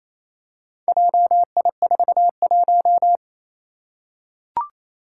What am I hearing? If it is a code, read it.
Morse code, heard as a keyed tone, about 29 wpm, 700 Hz. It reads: JI41
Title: Four Challenging Letter-Number Combinations 28wpm